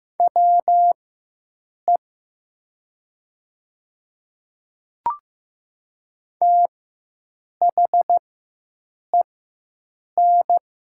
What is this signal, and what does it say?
Morse code, heard as a keyed tone, about 15 wpm, spaced as for 7 wpm, 700 Hz. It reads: WE THEN
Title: 4x Character Spacing - Top 300 Words - 15wpm